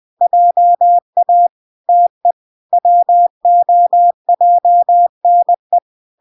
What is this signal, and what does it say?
Morse code, heard as a keyed tone, about 20 wpm, 700 Hz. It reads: JA TE WOJNE